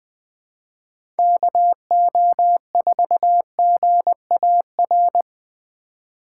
Morse code, heard as a keyed tone, about 21 wpm, 700 Hz. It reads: KO4GAR
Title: 2023 POTA Activator Call Signs - 20wpm